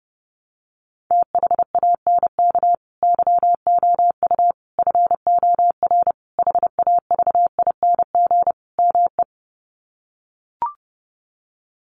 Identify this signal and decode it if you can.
Morse code, heard as a keyed tone, about 30 wpm, 700 Hz. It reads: THANK YOU FOR HAVING ME
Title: Sets of 5 Words Encore 30wpm